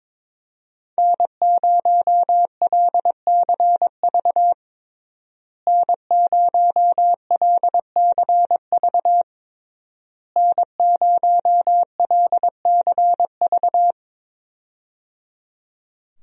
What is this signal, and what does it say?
Morse code, heard as a keyed tone, about 22 wpm, 700 Hz. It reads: N0LCV N0LCV N0LCV
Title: USA Call Signs - 5 Characters - Repeated 3x - 22wpm